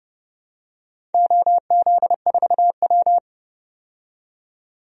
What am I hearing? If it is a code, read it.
Morse code, heard as a keyed tone, about 30 wpm, 700 Hz. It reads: OZ4W
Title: DXCC-100 Call Signs - 4 Characters - 30wpm